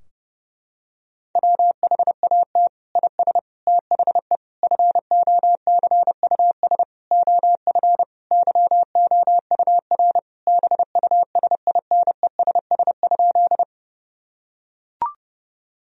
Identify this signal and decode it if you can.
Morse code, heard as a keyed tone, about 30 wpm, 700 Hz. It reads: WHAT IS THE FOCUS OF YOUR BUSINESS?
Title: Sentences from Top 1000 Words - Review Part 4 - 30wpm